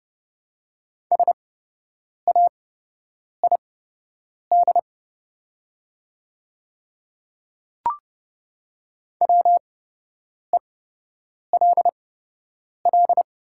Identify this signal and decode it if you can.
Morse code, heard as a keyed tone, about 29 wpm, spaced as for 8 wpm, 690 Hz. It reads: SAID WELL
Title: 8x Character Spacing - Top 200 Words - 30wpm